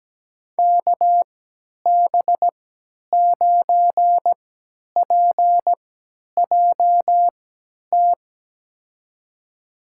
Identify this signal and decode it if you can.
Morse code, heard as a keyed tone, about 17 wpm, spaced as for 10 wpm, 700 Hz. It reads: KB9PJT
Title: USA Call Signs - 6 Characters - 3x Character Spacing - 17wpm